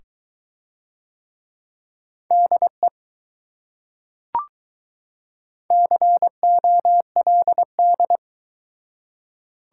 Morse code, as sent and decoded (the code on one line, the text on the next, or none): -.. / -.-. --- .-.. -..
DE COLD